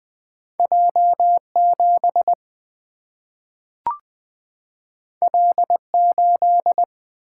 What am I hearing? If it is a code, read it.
Morse code, heard as a keyed tone, about 20 wpm, 700 Hz. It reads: J7 L8
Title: Single-Letter Single-Number 20wpm